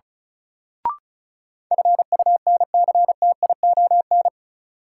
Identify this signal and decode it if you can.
Morse code, heard as a keyed tone, about 36 wpm, 700 Hz. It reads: FUNCTION
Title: Sentences from Top 1100 Words - New Word Drill Alt - 35wpm